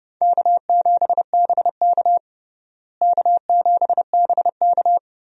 Morse code, heard as a keyed tone, about 30 wpm, 700 Hz. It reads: K7BK K7BK
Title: USA Call Signs - 4 Characters - Repeated 3x - 30wpm